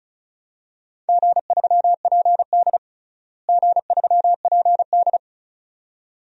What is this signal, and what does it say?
Morse code, heard as a keyed tone, about 36 wpm, 700 Hz. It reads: G3PD G3PD
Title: DXCC-100 Call Signs - 4 Characters - Repeated 2x - 35wpm